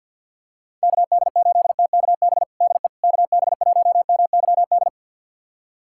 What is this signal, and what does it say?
Morse code, heard as a keyed tone, about 52 wpm, 700 Hz. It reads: KD8TKB DE KB1MYD
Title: USA Callsign DE Callsign - 6 Characters - 50wpm